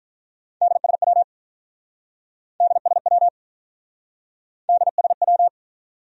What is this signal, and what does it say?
Morse code, heard as a keyed tone, about 48 wpm, 700 Hz. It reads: DSW DSW DSW